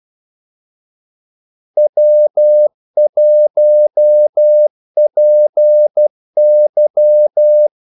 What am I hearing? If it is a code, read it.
Morse code, heard as a keyed tone, about 12 wpm, 600 Hz. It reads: W1PY
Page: 2 ekstra ordmellemrum mellem kaldesignalerne